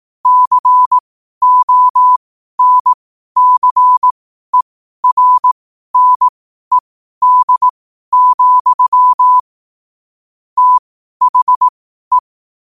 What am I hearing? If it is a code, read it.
Morse code, heard as a keyed tone, about 18 wpm, spaced as for 12 wpm, 1000 Hz. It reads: CONCERNED, THE
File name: qotdfiles / 12wpm